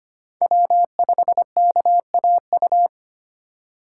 Morse code, heard as a keyed tone, about 25 wpm, 700 Hz. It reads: W5KAU
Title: USA Call Signs - 5 Characters - 4x Character Spacing - 25wpm